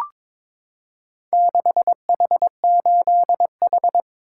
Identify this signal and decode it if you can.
Morse code, heard as a keyed tone, about 22 wpm, 700 Hz. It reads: 6H8H
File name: practice / 22wpm / Four Letter-Number Combinations - 22wpm